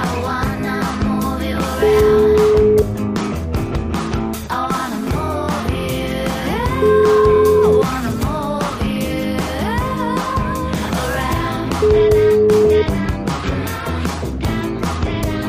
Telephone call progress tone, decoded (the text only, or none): ringback tone